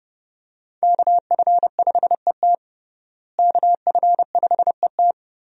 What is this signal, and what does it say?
Morse code, heard as a keyed tone, about 30 wpm, 700 Hz. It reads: KF5ET KF5ET
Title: USA Call Signs - 5 Characters - Repeated 2x - 30wpm